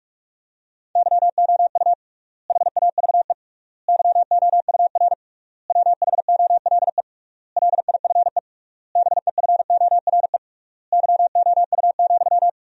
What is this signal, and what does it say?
Morse code, heard as a keyed tone, about 46 wpm, 700 Hz. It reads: YOU HAVE YOUR WHOLE LIFE BEFORE YOU,
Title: 3x Word-Spacing - Sentences from Top 400 Words - 45wpm